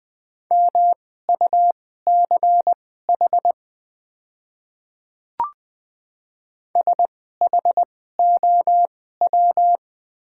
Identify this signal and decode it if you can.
Morse code, heard as a keyed tone, about 20 wpm, spaced as for 15 wpm, 700 Hz. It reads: MUCH SHOW
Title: 2x Character Spacing - Top 300 Words - 20wpm